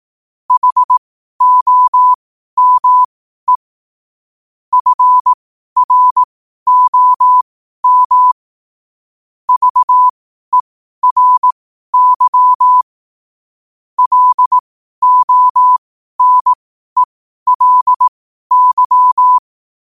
Morse code, heard as a keyed tone, about 18 wpm, spaced as for 12 wpm, 1000 Hz. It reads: HOME FROM VERY LONELY
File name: qotdfiles / 12wpm